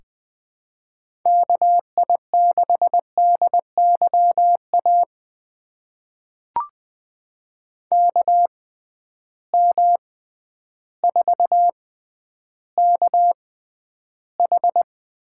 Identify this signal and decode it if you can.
Morse code, heard as a keyed tone, about 20 wpm, spaced as for 11 wpm, 700 Hz. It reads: KI6DYA KM4KH
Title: USA Call Signs - 6 Characters - 6x Character Spacing - 20wpm